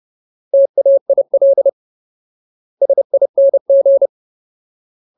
Morse code, heard as a keyed tone, about 30 wpm, 550 Hz. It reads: TAIL SING
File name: Ultimate Word List / 30WPM Ultimate